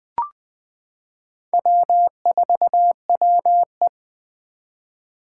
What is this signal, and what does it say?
Morse code, heard as a keyed tone, about 20 wpm, 700 Hz. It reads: W4WE
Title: Four Challenging Letter-Number Combinations 20wpm